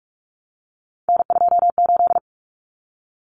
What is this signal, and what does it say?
Morse code, heard as a keyed tone, about 44 wpm, 700 Hz. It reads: N28
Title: Common Cut Numbers - 3-Digit 45wpm